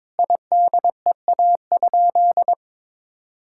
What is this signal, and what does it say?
Morse code, heard as a keyed tone, about 22 wpm, 700 Hz. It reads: IDEA?
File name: practice / 22wpm / Sentences with 8-Word Phrases - 22wpm